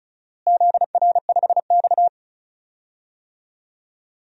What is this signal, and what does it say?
Morse code, heard as a keyed tone, about 34 wpm, 700 Hz. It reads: ZR5X
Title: DXCC-100 Call Signs - 4 Characters - Repeated 3x - 35wpm